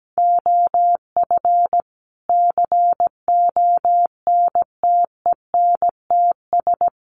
Morse code, heard as a keyed tone, about 17 wpm, 700 Hz. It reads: OF CONTENTS
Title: Sets of 5 Words Encore2 17wpm